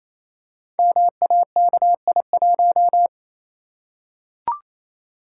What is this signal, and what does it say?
Morse code, heard as a keyed tone, about 28 wpm, 690 Hz. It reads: MAKI1
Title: Code Groups - 5 Letter:Numbers - 28wpm